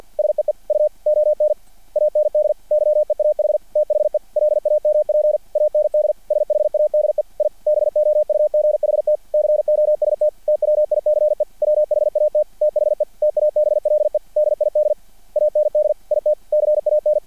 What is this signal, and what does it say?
Morse code, heard as a keyed tone, about 50 wpm, 600 Hz. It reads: SEE U ON AND YEAH THE LAMP AND SHADE I BOUGHT COST TWICE WHAT THE TABLE DID AND IT CAN
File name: archived / CRSnet